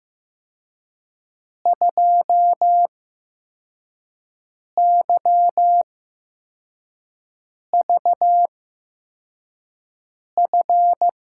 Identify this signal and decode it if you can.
Morse code, heard as a keyed tone, about 16 wpm, spaced as for 4 wpm, 700 Hz. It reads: 2YVF
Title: USA Call Signs - 6 Characters - 8x Character Spacing - 15wpm